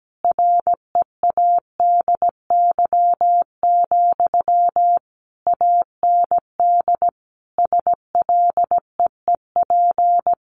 Morse code, heard as a keyed tone, about 17 wpm, 700 Hz. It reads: READY, AND SLEEP
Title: Sentences from Top 500 Words 17wpm